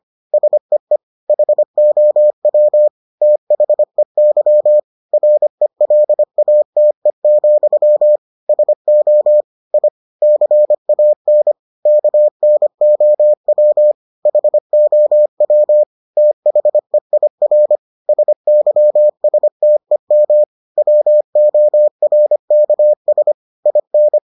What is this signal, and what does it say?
Morse code, heard as a keyed tone, about 25 wpm, 600 Hz. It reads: SEE HOW THEY RELATE, SO I CAN KNOW HOW THEIR SYSTEM WORKS IN